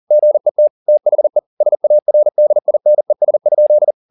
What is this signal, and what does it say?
Morse code, heard as a keyed tone, about 40 wpm, 600 Hz. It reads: GET THE SARDINES?